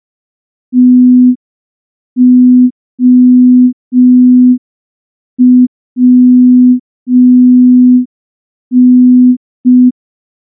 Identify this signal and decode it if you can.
Morse code, heard as a keyed tone, about 5 wpm, 250 Hz. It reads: TOWN
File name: morse / morse-training-05WPM